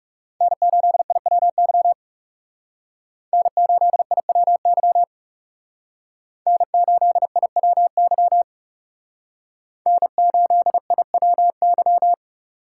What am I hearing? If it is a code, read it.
Morse code, heard as a keyed tone, about 36 wpm, 700 Hz. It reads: N8IWY N8IWY N8IWY N8IWY
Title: Speed Racing - USA 1x3 Call Signs 30wpm